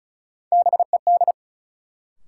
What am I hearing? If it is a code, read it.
Morse code, heard as a keyed tone, about 36 wpm, 700 Hz. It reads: BED
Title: Sentences from Top 500 Words - New Word Drill - 35wpm